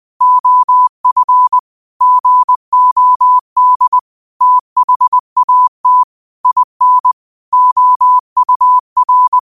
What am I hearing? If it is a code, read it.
Morse code, heard as a keyed tone, about 20 wpm, 1000 Hz. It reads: OF GOD THAT IN OUR